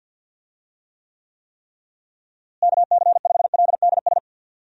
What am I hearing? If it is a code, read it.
Morse code, heard as a keyed tone, about 46 wpm, 700 Hz. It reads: KC5LDS
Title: USA Call Signs - 6 Characters - Repeated 3x - 50wpm